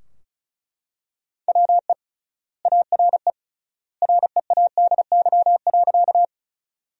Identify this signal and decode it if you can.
Morse code, heard as a keyed tone, about 36 wpm, 700 Hz. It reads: WE ARE READY.
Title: 3x Word-Spacing - Sentences from Top 400 Words - 35wpm